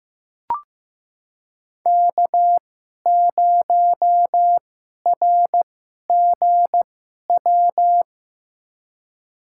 Morse code, heard as a keyed tone, about 15 wpm, spaced as for 11 wpm, 700 Hz. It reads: K0RGW
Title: USA Call Signs - 5 Characters - 2x Character Spacing - 15wpm